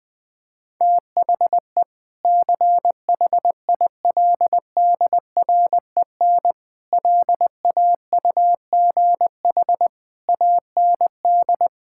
Morse code, heard as a keyed tone, about 20 wpm, 700 Hz. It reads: THE CHILDREN LAUGH AND
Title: Sentences with 2-Word Phrases - 20wpm